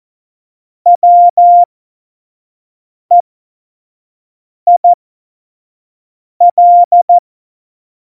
Morse code, heard as a keyed tone, about 14 wpm, spaced as for 5 wpm, 700 Hz. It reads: WEIL
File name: Morsekurs / Witze / Witz 29. August 17, Tempo 25 F70